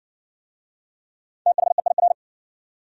Morse code, heard as a keyed tone, about 59 wpm, 700 Hz. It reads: THEIR